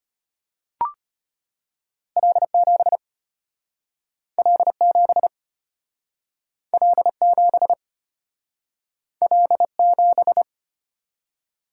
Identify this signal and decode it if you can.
Morse code, heard as a keyed tone, about 34 wpm, 700 Hz. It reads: L7 L7 L7 L7